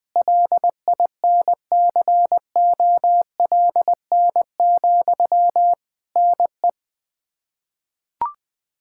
Morse code, heard as a keyed tone, about 21 wpm, 700 Hz. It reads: LINCOLN, NE
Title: Practice QSO - Exchange 1 - Beginning - 20wpm